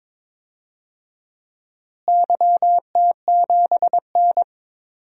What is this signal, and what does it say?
Morse code, heard as a keyed tone, about 22 wpm, 700 Hz. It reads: YT7N
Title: DXCC-100 Call Signs - 4 Characters - Repeated 3x - 22wpm